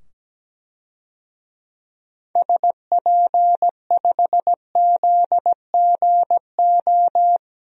Morse code, heard as a keyed tone, about 17 wpm, 700 Hz. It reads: SP5ZGO
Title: DXCC-100 Call Signs - 6 Characters - 17wpm